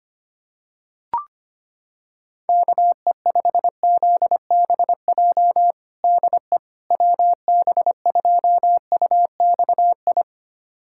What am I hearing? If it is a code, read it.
Morse code, heard as a keyed tone, about 25 wpm, 700 Hz. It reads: KE5ZBJ DE WB2UXI
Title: USA Callsign DE Callsign 25wpm